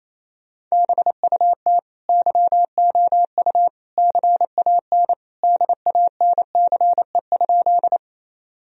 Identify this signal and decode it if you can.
Morse code, heard as a keyed tone, about 28 wpm, 700 Hz. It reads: BUT YOU CAN DANCE?
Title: Sentences from Top 1000 Words - Review Part 4 - 28wpm